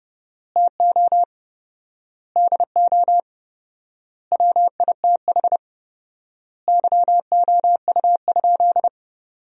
Morse code, heard as a keyed tone, about 30 wpm, 700 Hz. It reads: TO DO WITH YOU?
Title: 4x Word-Spacing - Sentences from Top 100 Words - 30wpm